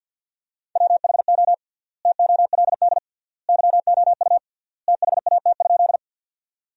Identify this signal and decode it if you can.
Morse code, heard as a keyed tone, about 50 wpm, 700 Hz. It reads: WHO TOLD YOU THAT?
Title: 3x Word-Spacing - Sentences from Top 400 Words - 50wpm